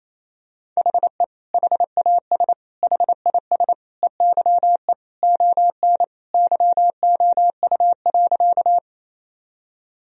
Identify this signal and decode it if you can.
Morse code, heard as a keyed tone, about 28 wpm, 700 Hz. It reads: HE HAS HIS EYE ON YOU.
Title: Sentences from Top 1000 Words - Review Part 1 - 28wpm